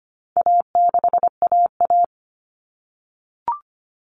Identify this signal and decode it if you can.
Morse code, heard as a keyed tone, about 25 wpm, 700 Hz. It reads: A6AA
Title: All Cut Numbers - 4-Digit 25wpm